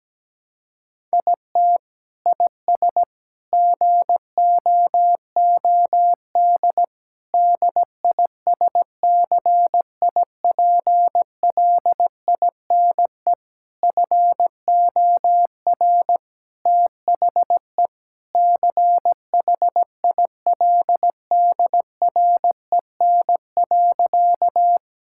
Morse code, heard as a keyed tone, about 17 wpm, 700 Hz. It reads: IT IS GOOD DISCIPLINE FOR THE CHILDREN.
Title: Sentences from Top 1000 Words - Review Part 6 - 17wpm